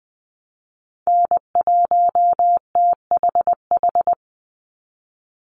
Morse code, heard as a keyed tone, about 20 wpm, 700 Hz. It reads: N1THH
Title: USA Call Signs 20wpm